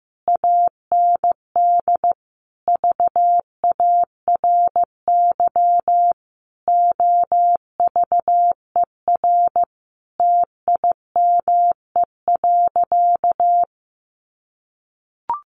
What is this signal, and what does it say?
Morse code, heard as a keyed tone, about 15 wpm, 700 Hz. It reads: AND VARY OVER TIME.
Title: Sentences from Top 1200 Words 15wpm